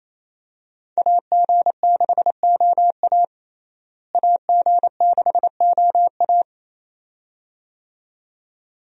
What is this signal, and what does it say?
Morse code, heard as a keyed tone, about 28 wpm, 700 Hz. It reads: AG6OA AG6OA
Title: USA Call Signs - 5 Characters - Repeated 2x - 28wpm